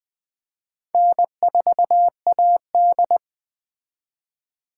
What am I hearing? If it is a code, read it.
Morse code, heard as a keyed tone, about 20 wpm, 700 Hz. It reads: N4AD
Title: Speed Racing - USA 1x2 Call Signs 17wpm